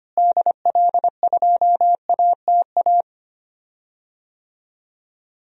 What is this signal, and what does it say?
Morse code, heard as a keyed tone, about 25 wpm, 700 Hz. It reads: DL2ATA
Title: DXCC-100 Call Signs - 6 Characters - Repeated 2x - 25wpm